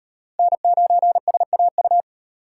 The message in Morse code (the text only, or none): N9SAU